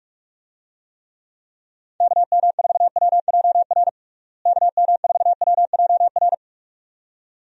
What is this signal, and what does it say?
Morse code, heard as a keyed tone, about 46 wpm, 700 Hz. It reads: KM4WJR KM4WJR
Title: USA Call Signs - 6 Characters - Repeated 2x - 45wpm